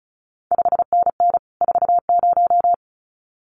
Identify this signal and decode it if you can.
Morse code, heard as a keyed tone, about 36 wpm, 700 Hz. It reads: 5NN 40
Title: CQ WW DX Contest Exchange Encore - 35wpm